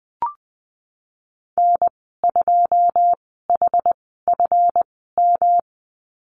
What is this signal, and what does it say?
Morse code, heard as a keyed tone, about 20 wpm, spaced as for 15 wpm, 700 Hz. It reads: N2HFM
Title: 193 Lesson 31g - Letter v - Call Signs 20wpm